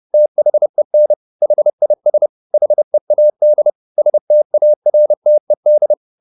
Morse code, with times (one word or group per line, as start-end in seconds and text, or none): 0.1-1.1 THEN
1.4-2.3 HIS
2.5-3.7 HEAD
4.0-5.9 STARTED